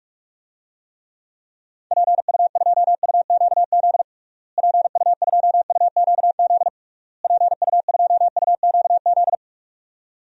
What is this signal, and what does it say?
Morse code, heard as a keyed tone, about 46 wpm, 700 Hz. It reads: PU2UQZ PU2UQZ PU2UQZ
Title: DXCC-100 Call Signs - 6 Characters - Repeated 3x - 45wpm